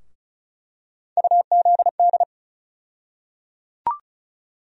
Morse code, heard as a keyed tone, about 34 wpm, 700 Hz. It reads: UZD
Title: USA Call Sign Postfixes - 3 Letters 35wpm